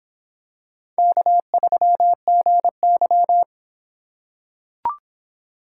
Morse code, heard as a keyed tone, about 25 wpm, 700 Hz. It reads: K3GY